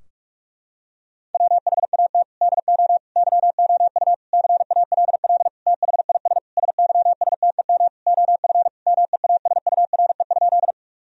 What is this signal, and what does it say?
Morse code, heard as a keyed tone, about 41 wpm, 700 Hz. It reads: WHAT DO YOU CALL THIS SYSTEM OF MEASURE?
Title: Sentences from Top 1000 Words - Review Part 3 - 45wpm